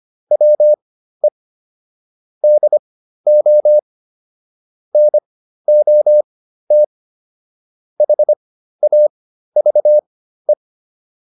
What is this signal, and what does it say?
Morse code, heard as a keyed tone, about 25 wpm, spaced as for 13 wpm, 600 Hz. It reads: WE DO NOT HAVE